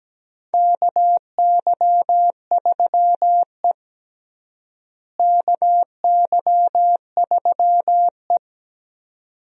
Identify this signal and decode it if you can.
Morse code, heard as a keyed tone, about 17 wpm, 700 Hz. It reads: KY3E KY3E
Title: USA Call Signs - 4 Characters - Repeated 2x - 17wpm